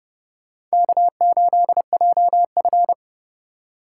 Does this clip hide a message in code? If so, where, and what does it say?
Yes: Morse code from 0.7 to 2.9 s, heard as a keyed tone, about 38 wpm, 700 Hz. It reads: K8JF